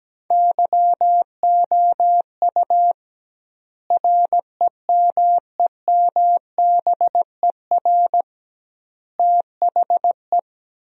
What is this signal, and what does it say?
Morse code, heard as a keyed tone, about 17 wpm, 700 Hz. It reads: YOU REMEMBER THE
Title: 2x Word-Spacing - Sentences from Top 400 Words - 17wpm